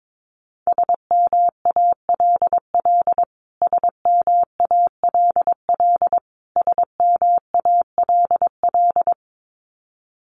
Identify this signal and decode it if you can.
Morse code, heard as a keyed tone, about 22 wpm, 700 Hz. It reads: SMALL SMALL SMALL
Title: Top 200 Words Repeated 22wpm